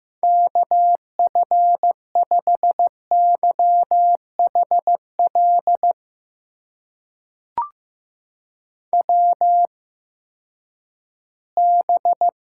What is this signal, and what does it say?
Morse code, heard as a keyed tone, about 15 wpm, spaced as for 10 wpm, 700 Hz. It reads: KF5YHL WB
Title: USA Call Signs - 6 Characters - 8x Character Spacing - 15wpm